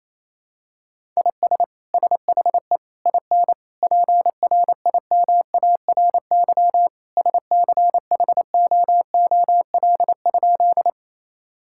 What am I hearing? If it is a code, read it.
Morse code, heard as a keyed tone, about 27 wpm, 700 Hz. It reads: IS SHE IN PRIMARY SCHOOL?
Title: Sentences from Top 1000 Words - Review Part 5 - 28wpm